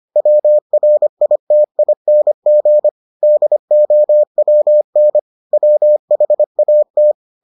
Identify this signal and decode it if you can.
Morse code, heard as a keyed tone, about 25 wpm, 600 Hz. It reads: WRITING DOWN WHAT